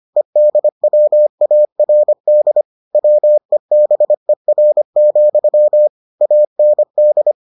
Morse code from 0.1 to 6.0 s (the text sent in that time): EDWARD WEBER,